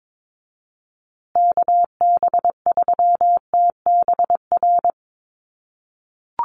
Morse code, heard as a keyed tone, about 22 wpm, 700 Hz. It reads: KB3TBR
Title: USA Call Signs Repeated 3x 22wpm